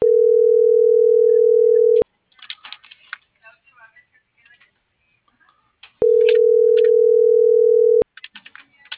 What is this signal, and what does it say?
Ringback tone (North America), 440 + 480 Hz.